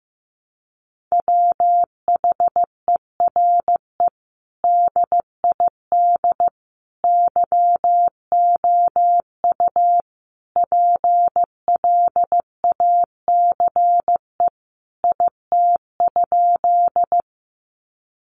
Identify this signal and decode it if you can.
Morse code, heard as a keyed tone, about 15 wpm, 700 Hz. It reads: WHERE DID YOU PLACE IT?
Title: Sentences from Top 200 Words 15wpm